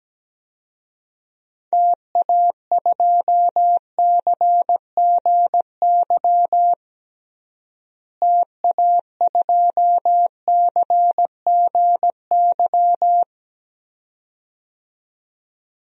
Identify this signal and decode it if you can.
Morse code, heard as a keyed tone, about 17 wpm, 700 Hz. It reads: TA2CGY TA2CGY
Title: DXCC-100 Call Signs - 6 Characters - Repeated 2x - 17wpm